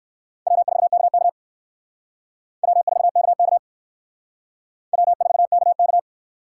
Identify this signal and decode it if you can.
Morse code, heard as a keyed tone, about 60 wpm, 700 Hz. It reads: W4KK W4KK W4KK